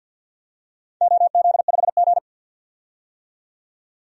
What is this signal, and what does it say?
Morse code, heard as a keyed tone, about 50 wpm, 700 Hz. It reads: OZ5G
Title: DXCC-100 Call Signs - 4 Characters - 50wpm